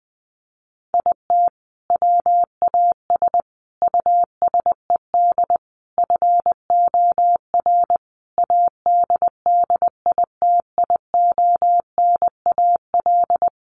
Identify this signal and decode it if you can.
Morse code, heard as a keyed tone, about 19 wpm, 700 Hz. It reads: IT WAS USED FOR ADDITIONAL
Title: Sentences from Top 1400 Words 20wpm